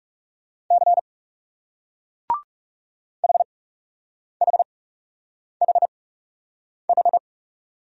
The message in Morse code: C HHHH

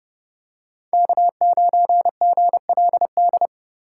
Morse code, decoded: K9GLD

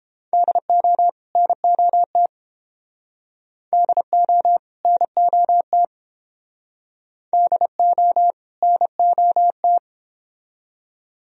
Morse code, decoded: DO NOT DO NOT DO NOT

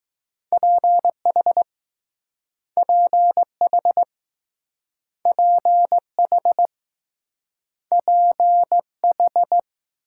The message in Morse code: PH PH PH PH

